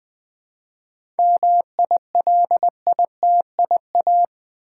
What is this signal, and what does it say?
Morse code, heard as a keyed tone, about 20 wpm, 700 Hz. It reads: MILITIA
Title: 7 Letter Words - First 3 Letters Given - 20wpm